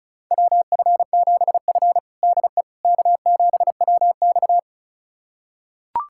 Morse code, heard as a keyed tone, about 36 wpm, 700 Hz. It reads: WF7F DE K7WX
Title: USA Callsign DE Callsign - 4 Characters - 35wpm